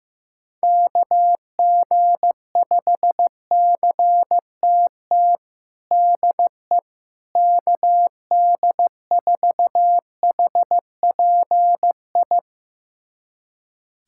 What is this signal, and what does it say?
Morse code, heard as a keyed tone, about 15 wpm, 700 Hz. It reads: KG5CTT DE KD4HPI
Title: USA Callsign DE Callsign 15wpm